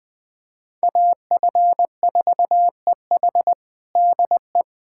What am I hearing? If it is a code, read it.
Morse code, heard as a keyed tone, about 20 wpm, 700 Hz. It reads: AF4EH DE
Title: USA Callsign DE Callsign - 5 Characters - 20wpm